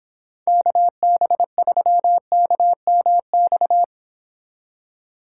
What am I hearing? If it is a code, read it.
Morse code, heard as a keyed tone, about 26 wpm, 700 Hz. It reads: KB3KMX